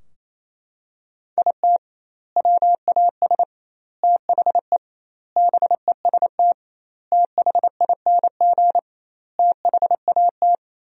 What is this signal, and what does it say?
Morse code, heard as a keyed tone, about 28 wpm, 700 Hz. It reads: IT WAS THE BEST THING THAT
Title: 2x Word-Spacing - Sentences from Top 400 Words - 28wpm